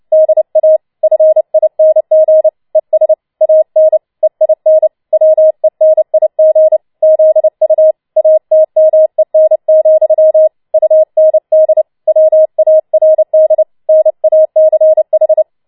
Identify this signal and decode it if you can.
Morse code, heard as a keyed tone, about 29 wpm, 630 Hz. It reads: DA FING ES AN EIN WENIG ZU ATMEN, UND WARD NACH
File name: Schneewittchen Teil3 30wpm